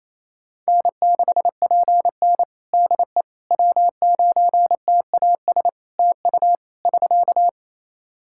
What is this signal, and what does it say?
Morse code, heard as a keyed tone, about 28 wpm, 700 Hz. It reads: N6PN DE W9TAS TU <SK>
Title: Practice QSO - Exchange 4 - Ending - 28wpm